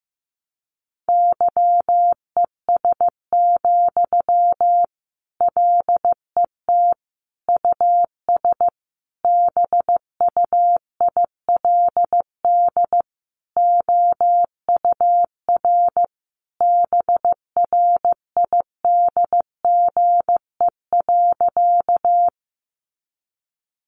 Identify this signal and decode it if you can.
Morse code, heard as a keyed tone, about 15 wpm, 700 Hz. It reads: YES, LET US BUILD OUR BRIDGE.
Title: Sentences from Top 1200 Words 15wpm